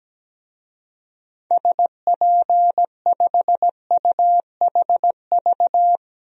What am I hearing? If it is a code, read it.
Morse code, heard as a keyed tone, about 17 wpm, 700 Hz. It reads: SP5UHV
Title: DXCC-100 Call Signs - 6 Characters - 17wpm